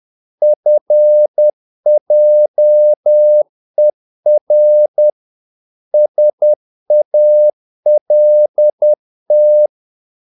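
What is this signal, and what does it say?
Morse code, heard as a keyed tone, about 10 wpm, 600 Hz. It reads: FJER SALT
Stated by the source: Morse practice of short Danish words at 10 wpm, from CW med Gnister.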